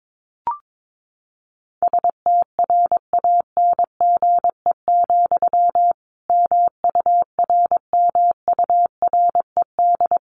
Morse code, heard as a keyed tone, about 22 wpm, 700 Hz. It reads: STRANGE, MURMURED
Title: A Sound Of Thunder 22wpm